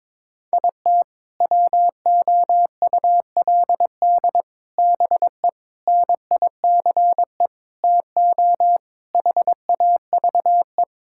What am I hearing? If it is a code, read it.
Morse code, heard as a keyed tone, about 21 wpm, 700 Hz. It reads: IT WOULD BE NICE TO HAVE
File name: practice / 22wpm / Sets of 7 Words - 22wpm